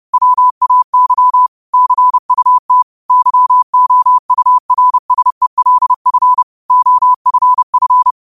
Morse code, heard as a keyed tone, about 30 wpm, 1000 Hz. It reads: WAY CUT YOURSELF OFF
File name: qotdfiles / 30wpm